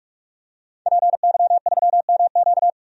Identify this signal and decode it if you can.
Morse code, heard as a keyed tone, about 46 wpm, 700 Hz. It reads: PY3MQ
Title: DXCC-100 Call Signs - 5 Characters - Repeated 2x - 45wpm